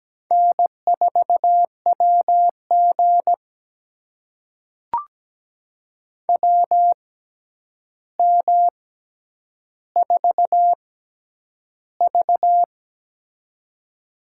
Morse code, heard as a keyed tone, about 17 wpm, spaced as for 9 wpm, 700 Hz. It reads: N4WG WM4V
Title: USA Call Signs - 4 Characters - 6x Character Spacing - 17wpm